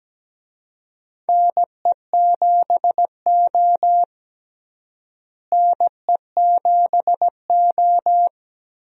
Morse code, heard as a keyed tone, about 17 wpm, 700 Hz. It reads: NE7O NE7O
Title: USA Call Signs - 4 Characters - Repeated 2x - 17wpm